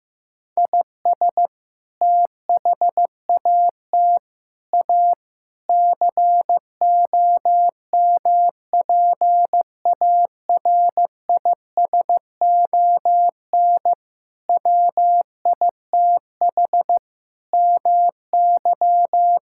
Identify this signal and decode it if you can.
Morse code, heard as a keyed tone, about 15 wpm, 700 Hz. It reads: IS THAT A COMPARISON WITH MY
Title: Sentences from Top 1600 Words 15wpm